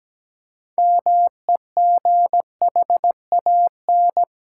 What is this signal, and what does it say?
Morse code, heard as a keyed tone, about 17 wpm, 700 Hz. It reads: MEGHAN
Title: Pangram Challenge v2 - 17wpm